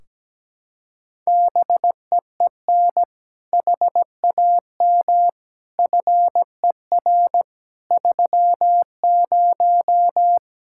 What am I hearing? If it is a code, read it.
Morse code, heard as a keyed tone, about 17 wpm, 700 Hz. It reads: BEEN HAM FER 30
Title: Practice QSO - Exchange 3 - More Info - 17wpm